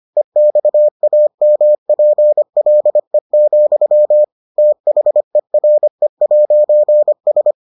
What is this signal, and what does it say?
Morse code, heard as a keyed tone, about 25 wpm, 600 Hz. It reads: EXAMPLE, THERE'S